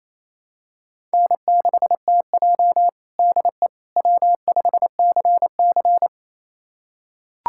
Morse code, heard as a keyed tone, about 28 wpm, 700 Hz. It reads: N6TJ DE W5CC
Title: USA Callsign DE Callsign - 4 Characters - 28wpm